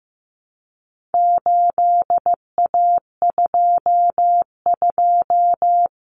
Morse code, heard as a keyed tone, about 15 wpm, 700 Hz. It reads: 8A22
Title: Common Cut Numbers - 4-Digit 15wpm